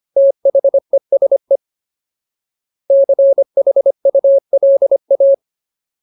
Morse code, heard as a keyed tone, about 25 wpm, 550 Hz. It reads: THESE CHULA
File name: Ultimate Word List / 25WPM Ultimate